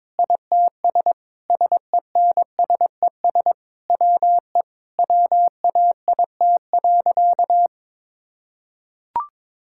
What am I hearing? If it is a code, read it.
Morse code, heard as a keyed tone, about 22 wpm, 700 Hz. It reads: ITS SENSES WE WAIT.
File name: Sentences with 9-Word Phrases - 22wpm